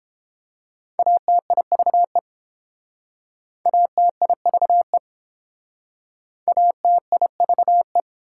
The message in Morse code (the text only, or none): ATIVE ATIVE ATIVE